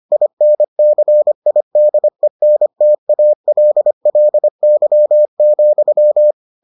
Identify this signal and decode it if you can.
Morse code, heard as a keyed tone, about 25 wpm, 600 Hz. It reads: INCIDENTALLY,